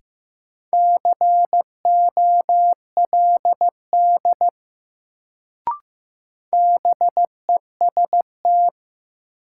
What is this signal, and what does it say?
Morse code, heard as a keyed tone, about 15 wpm, 700 Hz. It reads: COLD BEST
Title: Sentences from Top 400 Words - New Word Drill - 15wpm